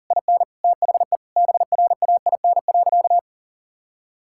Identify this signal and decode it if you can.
Morse code, heard as a keyed tone, about 38 wpm, 700 Hz. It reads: IN THE BRAIN.
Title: Sentences from Top 1000 Words - Review Part 6 - 40wpm